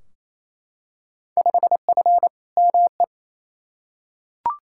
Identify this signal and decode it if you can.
Morse code, heard as a keyed tone, about 28 wpm, 700 Hz. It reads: 5F ME
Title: Field Day Exchanges Encore - 28wpm